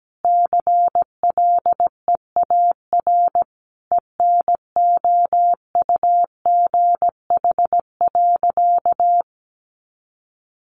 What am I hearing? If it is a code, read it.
Morse code, heard as a keyed tone, about 17 wpm, 700 Hz. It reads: CLEAR ENOUGH.
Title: Sentences from Top 1500 Words 17wpm